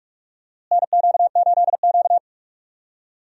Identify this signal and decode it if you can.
Morse code, heard as a keyed tone, about 46 wpm, 700 Hz. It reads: NQ8Q
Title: USA Call Signs - 4 Characters - Repeated 3x - 45wpm